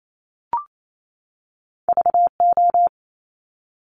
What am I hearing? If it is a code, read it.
Morse code, heard as a keyed tone, about 28 wpm, 700 Hz. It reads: VO
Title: Top Two Letter Combinations in Words 28wpm